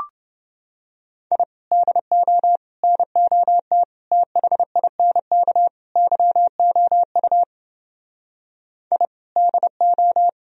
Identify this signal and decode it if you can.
Morse code, heard as a keyed tone, about 29 wpm, 700 Hz. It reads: I DO NOT THINK YOU I DO